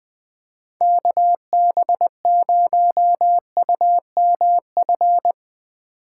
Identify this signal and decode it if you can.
Morse code, heard as a keyed tone, about 20 wpm, 700 Hz. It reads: KB0UMF
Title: USA Call Signs - 6 Characters - 2x Character Spacing - 20wpm